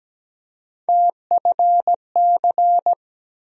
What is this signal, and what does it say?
Morse code, heard as a keyed tone, about 17 wpm, 700 Hz. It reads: TFC
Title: USA Call Sign Postfixes - 3 Letters 17wpm